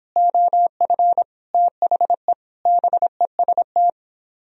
Morse code, heard as a keyed tone, about 31 wpm, 700 Hz. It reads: OF THE BEST